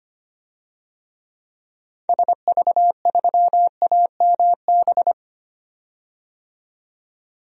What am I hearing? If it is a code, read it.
Morse code, heard as a keyed tone, about 25 wpm, 700 Hz. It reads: SV3AMB
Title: DXCC-100 Call Signs - 6 Characters - 25wpm